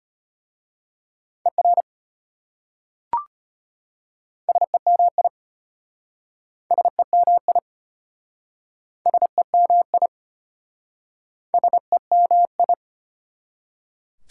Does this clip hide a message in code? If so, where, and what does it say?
Yes: Morse code from 1.5 to 12.7 s, heard as a keyed tone, about 33 wpm, 700 Hz. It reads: ER SEMI SEMI SEMI SEMI